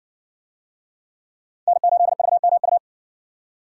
Morse code, heard as a keyed tone, about 55 wpm, 700 Hz. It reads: N8VMU